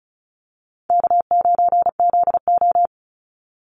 Morse code, heard as a keyed tone, about 34 wpm, 700 Hz. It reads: K9ZO
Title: Easy Call Signs Repeated 2x - 35wpm